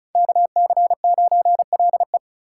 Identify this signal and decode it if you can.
Morse code, heard as a keyed tone, about 34 wpm, 700 Hz. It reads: KC9LE
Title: USA Call Signs - 5 Characters - 4x Character Spacing - 35wpm